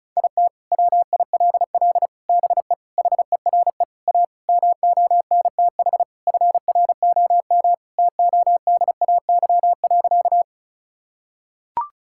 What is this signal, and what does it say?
Morse code, heard as a keyed tone, about 36 wpm, 700 Hz. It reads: IT WILL BE HERE A MONTH FROM TODAY.
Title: Sentences from Top 1000 Words - Review Part 5 - 35wpm